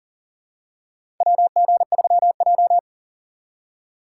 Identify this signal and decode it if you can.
Morse code, heard as a keyed tone, about 37 wpm, 700 Hz. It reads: WG3J
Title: USA Call Signs - 4 Characters - 4x Character Spacing - 40wpm